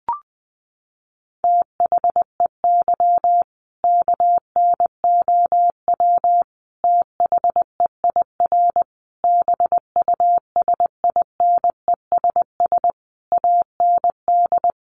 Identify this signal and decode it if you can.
Morse code, heard as a keyed tone, about 20 wpm, 700 Hz. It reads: THEY KNOW THEIR BUSINESS AND
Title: Sentences from Top 1200 Words 20wpm